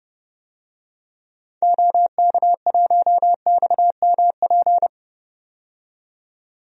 Morse code, heard as a keyed tone, about 30 wpm, 700 Hz. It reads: OK1XMP
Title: DXCC-100 Call Signs - 6 Characters - 30wpm